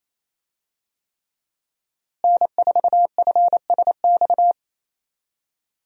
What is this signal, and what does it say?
Morse code, heard as a keyed tone, about 28 wpm, 700 Hz. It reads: N4FSX
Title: USA Call Signs - 5 Characters - Repeated 3x - 28wpm